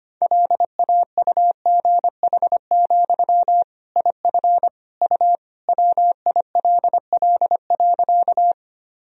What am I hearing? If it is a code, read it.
Morse code, heard as a keyed tone, about 25 wpm, 700 Hz. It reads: LAUGH, IF U WILL.
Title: Sentences from Top 500 Words and Abbreviations 25wpm